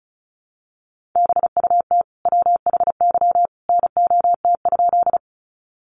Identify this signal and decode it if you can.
Morse code, heard as a keyed tone, about 34 wpm, 700 Hz. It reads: BUT WHY NOT?
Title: Sentences from Top 600 Words 35wpm